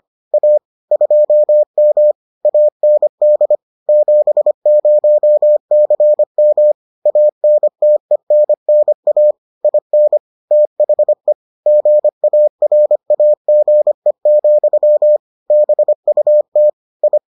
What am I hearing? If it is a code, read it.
Morse code, heard as a keyed tone, about 25 wpm, 600 Hz. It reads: A 2M AND 70CM ANTENNA IN THE GARAGE, BUT I